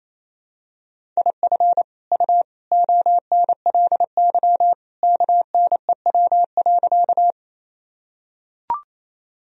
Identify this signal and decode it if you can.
Morse code, heard as a keyed tone, about 28 wpm, 700 Hz. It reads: IF U ONLY KNEW.
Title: Sentences from Top 400 Words and Abbreviations 28wpm